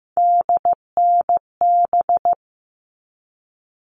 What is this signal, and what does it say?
Morse code, heard as a keyed tone, about 15 wpm, 700 Hz. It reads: DNB
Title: All Cut Numbers - 3-Digit 15wpm